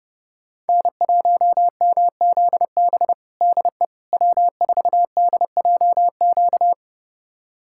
Morse code, heard as a keyed tone, about 30 wpm, 700 Hz. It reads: N1MZB DE W4DJQ
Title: USA Callsign DE Callsign - 5 Characters - 30wpm